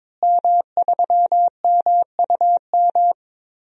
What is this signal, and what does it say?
Morse code, heard as a keyed tone, about 22 wpm, 700 Hz. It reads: M3MUM
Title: DXCC-100 Call Signs - 5 Characters - Repeated 2x - 22wpm